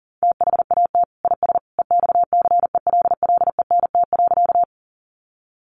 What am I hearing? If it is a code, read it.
Morse code, heard as a keyed tone, about 40 wpm, 700 Hz. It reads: THAT IS EXCELLENT.
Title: Sentences from Top 1000 Words 40wpm